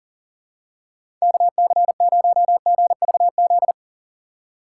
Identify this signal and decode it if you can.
Morse code, heard as a keyed tone, about 40 wpm, 700 Hz. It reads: KC0GVZ
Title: USA Call Signs - 6 Characters - 4x Character Spacing - 40wpm